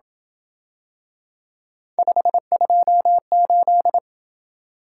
Morse code, heard as a keyed tone, about 27 wpm, 700 Hz. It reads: 528